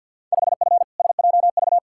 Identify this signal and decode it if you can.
Morse code, heard as a keyed tone, about 53 wpm, 700 Hz. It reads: 5F SJV
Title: Field Day Exchanges Encore - 50wpm